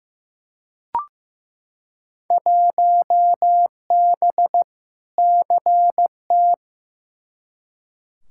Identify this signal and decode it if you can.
Morse code, heard as a keyed tone, about 15 wpm, 700 Hz. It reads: 1B CT
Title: Field Day Exchanges Encore - 15wpm